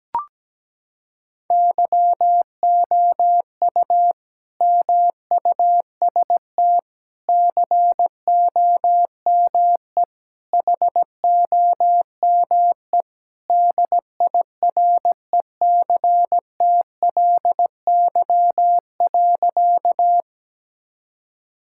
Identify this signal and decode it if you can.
Morse code, heard as a keyed tone, about 17 wpm, 700 Hz. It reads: YOU MUST COME HOME DIRECTLY.
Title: Sentences from Top 1000 Words - Review Part 5 - 17wpm